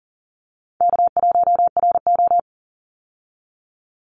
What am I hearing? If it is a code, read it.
Morse code, heard as a keyed tone, about 40 wpm, 700 Hz. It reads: K1RO
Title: Call Signs Repeated 3x - 40wpm